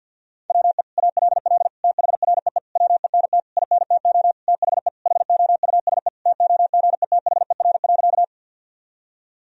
Morse code, heard as a keyed tone, about 52 wpm, 700 Hz. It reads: WE ALL THREE WENT INTO THE HOUSE TOGETHER.
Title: Sentences from Top 1000 Words - Review Part 1 - 50wpm